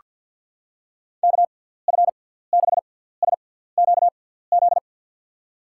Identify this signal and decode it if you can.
Morse code, heard as a keyed tone, about 48 wpm, spaced as for 17 wpm, 700 Hz. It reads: KF6SQZ
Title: USA Call Signs - 6 Characters - 6x Character Spacing - 50wpm